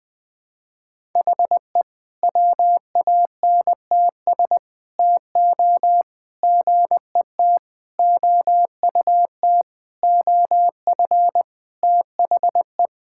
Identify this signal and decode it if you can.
Morse code, heard as a keyed tone, about 21 wpm, 700 Hz. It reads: HE WANTS TO GET OUT OF THE
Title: Sentences with 4-Word Phrases - 20wpm